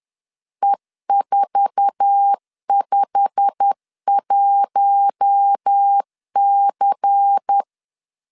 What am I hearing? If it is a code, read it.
Morse code, heard as a keyed tone, about 10 wpm, 800 Hz. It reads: E451C
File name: LEZIONI / 8 YJ45 / 8 YJ45 10wpm 2549